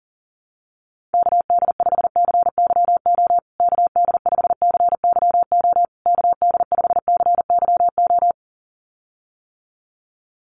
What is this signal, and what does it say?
Morse code, heard as a keyed tone, about 40 wpm, 700 Hz. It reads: KD5CYO KD5CYO KD5CYO
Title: USA Call Signs Repeated 3x 40wpm